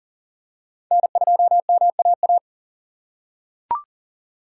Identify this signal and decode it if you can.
Morse code, heard as a keyed tone, about 38 wpm, 700 Hz. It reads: N2MAA
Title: USA Call Signs - 5 Characters - 6x Character Spacing - 40wpm